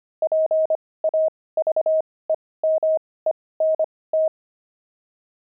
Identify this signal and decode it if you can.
Morse code, heard as a keyed tone, about 25 wpm, spaced as for 18 wpm, 630 Hz. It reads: PAVEMENT
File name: pangram-fs-18-cs-25-pavement